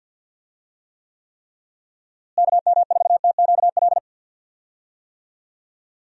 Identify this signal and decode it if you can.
Morse code, heard as a keyed tone, about 51 wpm, 700 Hz. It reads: KM4TQL
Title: USA Call Signs - 6 Characters - Repeated 3x - 50wpm